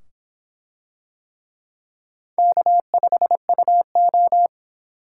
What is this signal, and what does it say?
Morse code, heard as a keyed tone, about 26 wpm, 700 Hz. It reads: K5UO